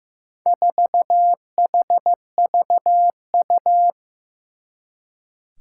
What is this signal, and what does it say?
Morse code, heard as a keyed tone, about 15 wpm, 700 Hz. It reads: 4HVU
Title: Four Challenging Letter-Number Combinations 15wpm